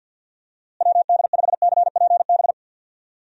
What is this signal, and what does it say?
Morse code, heard as a keyed tone, about 51 wpm, 700 Hz. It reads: WD5CPB
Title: USA Call Signs - 6 Characters - Repeated 2x - 50wpm